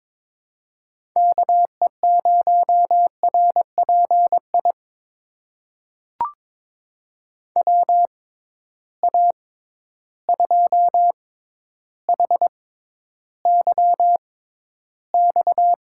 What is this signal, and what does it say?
Morse code, heard as a keyed tone, about 22 wpm, spaced as for 11 wpm, 700 Hz. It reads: KE0RPI WA2HYX
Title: USA Call Signs - 6 Characters - 6x Character Spacing - 22wpm